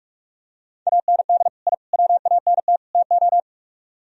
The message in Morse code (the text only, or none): AND I WANT TO